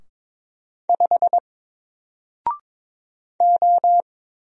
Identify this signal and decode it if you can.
Morse code, heard as a keyed tone, about 22 wpm, 700 Hz. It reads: 5 O
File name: Single-Letter-Number-22wpm